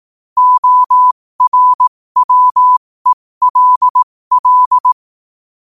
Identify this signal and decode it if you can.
Morse code, heard as a keyed tone, about 18 wpm, spaced as for 15 wpm, 1000 Hz. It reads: ORWELL
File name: qotdfiles / 15wpm